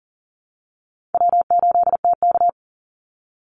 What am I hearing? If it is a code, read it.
Morse code, heard as a keyed tone, about 39 wpm, 700 Hz. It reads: W8TK
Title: Easy Call Signs Repeated 3x - 40wpm